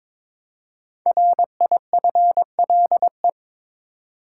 Morse code, heard as a keyed tone, about 22 wpm, 700 Hz. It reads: RIFLE
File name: 5 Letter Words - First 3 Letters Given - 22wpm